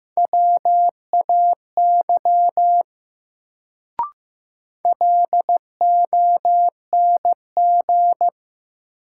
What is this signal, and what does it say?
Morse code, heard as a keyed tone, about 15 wpm, 700 Hz. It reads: WAY LONG
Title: Sets of 2 Words - New Word Drill - 15wpm